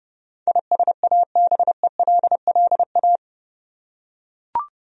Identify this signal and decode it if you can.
Morse code, heard as a keyed tone, about 30 wpm, 700 Hz. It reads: ISABELLA
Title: USA Common Names - 30wpm